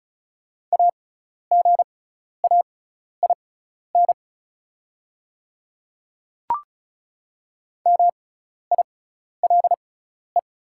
Morse code, heard as a keyed tone, about 36 wpm, spaced as for 12 wpm, 700 Hz. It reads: AGAIN MILE
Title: 6x Character Spacing - Top 500 Words - 35wpm